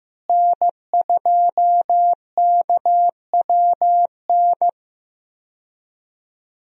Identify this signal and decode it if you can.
Morse code, heard as a keyed tone, about 16 wpm, 700 Hz. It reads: N2KWN
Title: USA Call Signs - 5 Characters - Repeated 3x - 15wpm